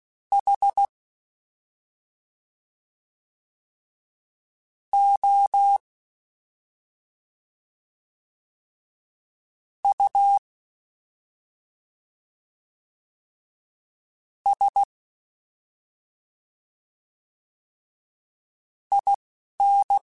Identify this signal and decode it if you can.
Morse code, heard as a keyed tone, about 16 wpm, spaced as for 3 wpm, 800 Hz. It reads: HOUSIN
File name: Special INGs-mp3s / Special INGs-mp3s / Slow ing @10